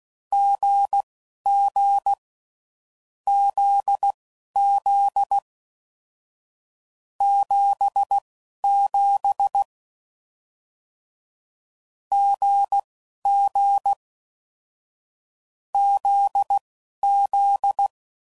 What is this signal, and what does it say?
Morse code, heard as a keyed tone, about 16 wpm, spaced as for 11 wpm, 800 Hz. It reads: GG ZZ 77 GG ZZ
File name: Combinations / Difficult Combos / GZ GZ7